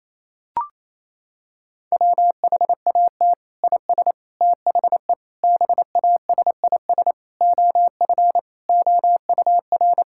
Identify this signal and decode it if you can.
Morse code, heard as a keyed tone, about 28 wpm, 700 Hz. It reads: WHAT IS THE BASIS OF OUR
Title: Sentences from Top 1000 Words - Review Part 5 - 28wpm